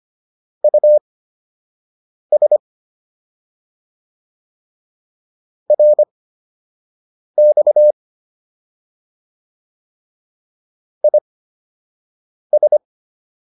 Morse code, heard as a keyed tone, about 25 wpm, spaced as for 6 wpm, 600 Hz. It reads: US RX IS